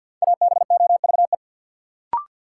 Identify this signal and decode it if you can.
Morse code, heard as a keyed tone, about 52 wpm, 700 Hz. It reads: ABOVE
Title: Sentences from Top 1200 Words - New Word Drill Alt - 50wpm